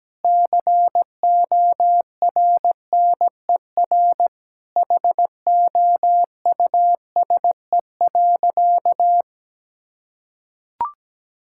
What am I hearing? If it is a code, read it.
Morse code, heard as a keyed tone, about 17 wpm, 700 Hz. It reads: CORNER HOUSE.
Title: Sentences from Top 1600 Words 17wpm